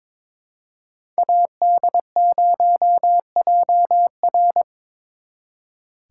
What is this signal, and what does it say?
Morse code, heard as a keyed tone, about 26 wpm, 700 Hz. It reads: AD0JR
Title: Speed Racing - USA 2x2 Call Signs 22wpm